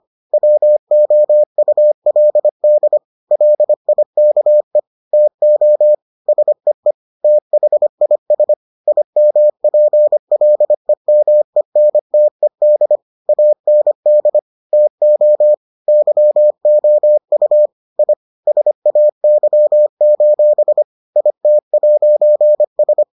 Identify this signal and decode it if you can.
Morse code, heard as a keyed tone, about 25 wpm, 600 Hz. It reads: WOULD LIKE TO SEE THIS IMPLEMENTED AND TO YOU I SAY: IT'S